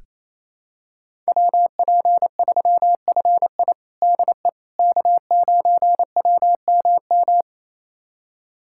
Morse code, heard as a keyed tone, about 28 wpm, 700 Hz. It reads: WP3FI DE K9WMM
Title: USA Callsign DE Callsign - 5 Characters - 28wpm